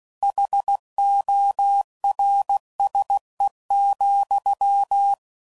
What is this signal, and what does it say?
Morse code, heard as a keyed tone, about 16 wpm, 800 Hz. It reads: HORSE,